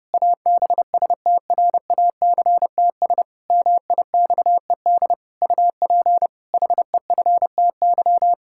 Morse code, heard as a keyed tone, about 30 wpm, 700 Hz. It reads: ABSTRACTS MIXED UP HEFTY
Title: Pangram Challenge v2 - 30wpm